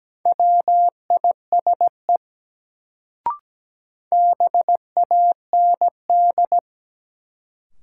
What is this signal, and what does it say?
Morse code, heard as a keyed tone, about 17 wpm, 700 Hz. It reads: WISE BAND
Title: Sentences from Top 1600 Words - New Word Drill Alt - 17wpm